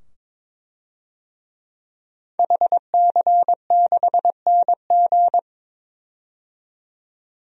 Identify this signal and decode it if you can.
Morse code, heard as a keyed tone, about 22 wpm, 700 Hz. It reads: HC6NG
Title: DXCC-100 Call Signs - 5 Characters - 22wpm